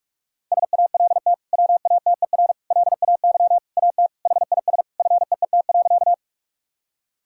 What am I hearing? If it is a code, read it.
Morse code, heard as a keyed tone, about 42 wpm, 690 Hz. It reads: SALT WATER LAY AT HIS FEET.
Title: Sentences from Top 1000 Words - Review Part 5 - 45wpm